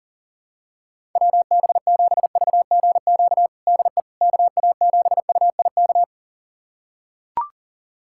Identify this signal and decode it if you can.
Morse code, heard as a keyed tone, about 40 wpm, 700 Hz. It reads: WB7VGQ DE KA7UIK
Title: USA Callsign DE Callsign - 6 Characters - 40wpm